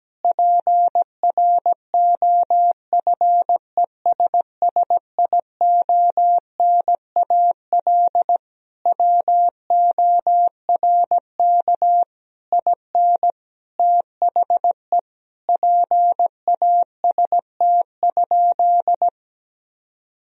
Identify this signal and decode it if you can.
Morse code, heard as a keyed tone, about 17 wpm, 700 Hz. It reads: PROFESSIONAL WORK IN THE PAST?
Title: Sentences from Top 1000 Words - Review Part 4 - 17wpm